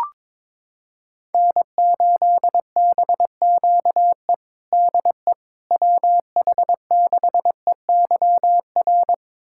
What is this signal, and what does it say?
Morse code, heard as a keyed tone, about 22 wpm, 700 Hz. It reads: N8BQE DE WH6EYR
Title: USA Callsign DE Callsign 22wpm